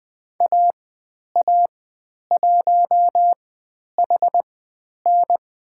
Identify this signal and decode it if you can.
Morse code, heard as a keyed tone, about 20 wpm, spaced as for 10 wpm, 700 Hz. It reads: AA1HN
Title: 095 Lesson 16g - Letter c - Call Signs 20wpm